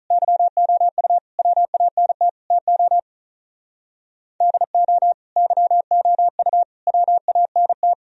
Morse code, heard as a keyed tone, about 43 wpm, 700 Hz. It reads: YOU WANT TO DO YOU WANT
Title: Speed-Racing - Sets of 4 Words 35wpm